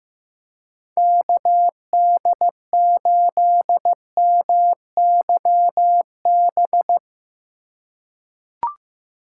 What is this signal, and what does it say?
Morse code, heard as a keyed tone, about 15 wpm, 700 Hz. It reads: KD8MYB
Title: USA Call Signs - 6 Characters - 4x Character Spacing - 15wpm